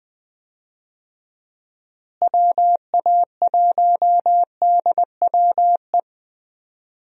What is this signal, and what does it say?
Morse code, heard as a keyed tone, about 20 wpm, 700 Hz. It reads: WA1DWE
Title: USA Call Signs - 6 Characters - Repeated 3x - 20wpm